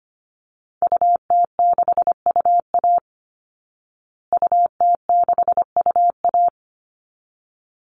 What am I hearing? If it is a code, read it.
Morse code, heard as a keyed tone, about 25 wpm, 700 Hz. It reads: UT6UA UT6UA
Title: Call Signs Repeated 3x - 25wpm